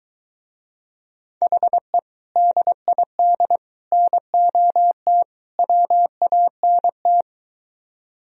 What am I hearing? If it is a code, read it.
Morse code, heard as a keyed tone, about 22 wpm, 700 Hz. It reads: HE DID NOT WANT